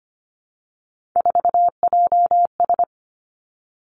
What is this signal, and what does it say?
Morse code, heard as a keyed tone, about 25 wpm, 700 Hz. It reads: 4JS
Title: Three Challenging Letter-Number Combinations 25wpm